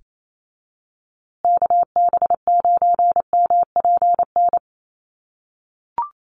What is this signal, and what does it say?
Morse code, heard as a keyed tone, about 28 wpm, 700 Hz. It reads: KB9MPN
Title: USA Call Signs Repeated 3x 28wpm